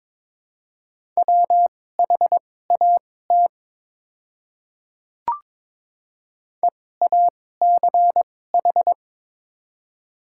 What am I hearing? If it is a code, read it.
Morse code, heard as a keyed tone, about 22 wpm, spaced as for 16 wpm, 700 Hz. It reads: WHAT EACH
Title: 2x Character Spacing - Top 100 Words - 22wpm